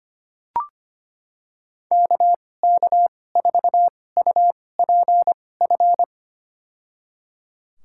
Morse code, heard as a keyed tone, about 25 wpm, spaced as for 18 wpm, 700 Hz. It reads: KK4UPF
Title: USA Call Signs - 6 Characters - 2x Character Spacing - 25wpm